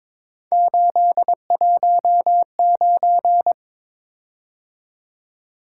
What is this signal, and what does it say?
Morse code, heard as a keyed tone, about 23 wpm, 700 Hz. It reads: 819
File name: practice / 22wpm / Three Letter-Number Combinations - 22wpm